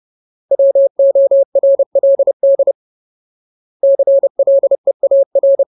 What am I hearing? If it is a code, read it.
Morse code, heard as a keyed tone, about 30 wpm, 550 Hz. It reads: WORLD CLEAR
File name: Ultimate Word List / 30WPM Ultimate